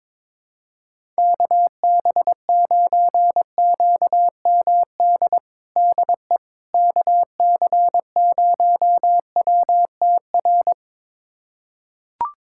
Morse code, heard as a keyed tone, about 21 wpm, 700 Hz. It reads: KB9QMD DE KC0WTR
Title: USA Callsign DE Callsign 22wpm